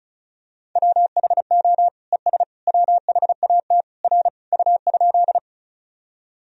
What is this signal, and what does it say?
Morse code, heard as a keyed tone, about 36 wpm, 700 Hz. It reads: WHO ES WHAT R U?
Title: Sentences from Top 100 Words and Abbreviations 35wpm